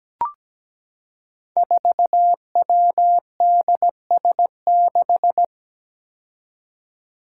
Morse code, heard as a keyed tone, about 17 wpm, 700 Hz. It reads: 4WDS6
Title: Five Challenging Letter-Number Combinations 17wpm